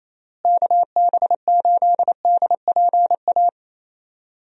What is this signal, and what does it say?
Morse code, heard as a keyed tone, about 28 wpm, 700 Hz. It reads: KB8DPA
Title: USA Call Signs - 6 Characters - Repeated 2x - 28wpm